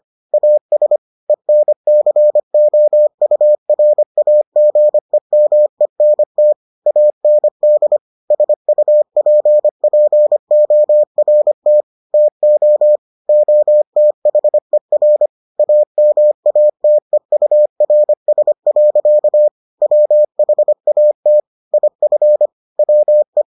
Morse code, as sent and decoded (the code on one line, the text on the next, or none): .- ... / . -. -.-. --- ..- .-. .- --. -- . -. - / .- -. -.. / ... ..- .--. .--. --- .-. - / - --- / --- - .... .-. / .- -- .- - . ..- .-. ... .-.-.- / .-- .... .- - / .. ..-. / .-- .
AS ENCOURAGEMENT AND SUPPORT TO OTHER AMATEURS. WHAT IF WE